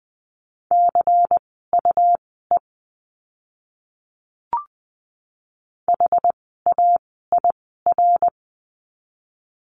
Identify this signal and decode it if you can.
Morse code, heard as a keyed tone, about 20 wpm, spaced as for 14 wpm, 700 Hz. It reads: CUE HAIR
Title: 101 Lesson 17f - Letter u - Words 20wpm